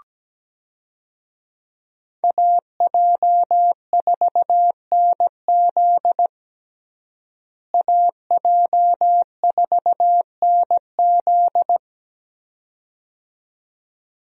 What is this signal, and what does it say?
Morse code, heard as a keyed tone, about 17 wpm, 700 Hz. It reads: AJ4NZ AJ4NZ
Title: USA Call Signs - 5 Characters - Repeated 2x - 17wpm